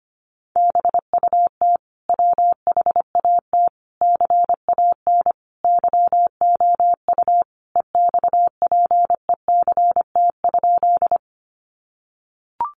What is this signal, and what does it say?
Morse code, heard as a keyed tone, about 25 wpm, 700 Hz. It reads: BUT WHAT CAN YOU EXPECT?
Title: Sentences from Top 1100 Words 25wpm